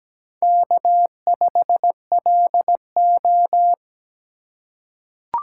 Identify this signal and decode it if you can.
Morse code, heard as a keyed tone, about 17 wpm, 700 Hz. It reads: K5LO
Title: USA Call Signs - 2x Character Spacing - 17wpm